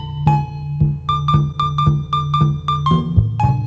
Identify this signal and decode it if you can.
There is no code.